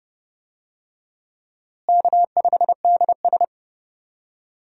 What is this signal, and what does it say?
Morse code, heard as a keyed tone, about 30 wpm, 700 Hz. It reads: K5DS
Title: USA Call Signs - 4 Characters - Repeated 3x - 30wpm